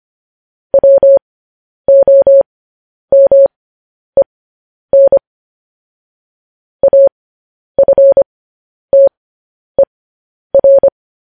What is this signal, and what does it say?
Morse code, heard as a keyed tone, about 25 wpm, spaced as for 10 wpm, 550 Hz. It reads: WOMEN AFTER